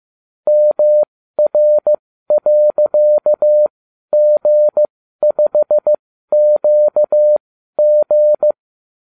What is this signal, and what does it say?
Morse code, heard as a keyed tone, about 15 wpm, spaced as for 12 wpm, 600 Hz. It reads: MR.G5QG